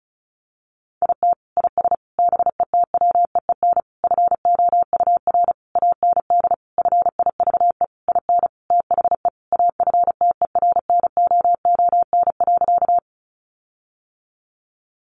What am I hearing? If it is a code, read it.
Morse code, heard as a keyed tone, about 34 wpm, 700 Hz. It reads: IT IS BETWEEN FOUR AND FIVE IN THE AFTERNOON.
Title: Sentences from Top 1500 Words 35wpm